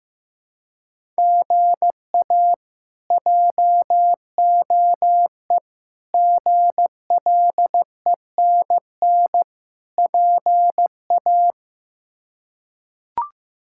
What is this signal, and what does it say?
Morse code, heard as a keyed tone, about 15 wpm, 700 Hz. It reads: GA JOE GLENN PA
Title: K1USN SST QSO Practice - 15wpm